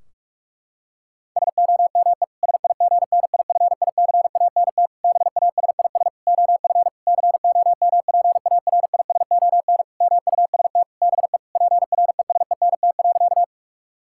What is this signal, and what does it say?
Morse code, heard as a keyed tone, about 46 wpm, 700 Hz. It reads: SOME SIGNIFICANT BASIS OF COMPARISON MUST BE PRESENT.
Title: Sentences from Top 1600 Words 45wpm